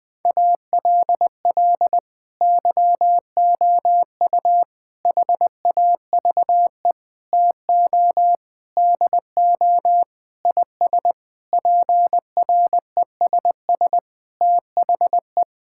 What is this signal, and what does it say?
Morse code, heard as a keyed tone, about 20 wpm, 700 Hz. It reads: ALL YOU HAVE TO DO IS PRESS THE
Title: Sentences with 6-Word Phrases - 20wpm